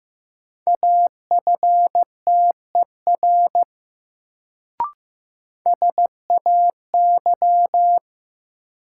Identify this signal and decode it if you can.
Morse code, heard as a keyed tone, about 15 wpm, 700 Hz. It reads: AFTER SAY
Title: Sentences from Top 200 Words - New Word Drill - 15wpm